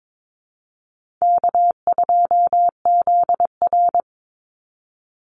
Morse code, heard as a keyed tone, about 22 wpm, 700 Hz. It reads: K2ZR
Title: Easy Call Signs - 22wpm